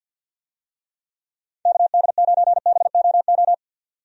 Morse code, heard as a keyed tone, about 50 wpm, 700 Hz. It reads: KD9BOO
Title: USA Call Signs - 6 Characters - 50wpm